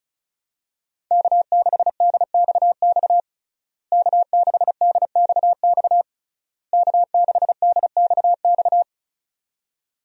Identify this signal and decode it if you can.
Morse code, heard as a keyed tone, about 36 wpm, 700 Hz. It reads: K6DXX K6DXX K6DXX
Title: USA Call Signs - 5 Characters - Repeated 3x - 35wpm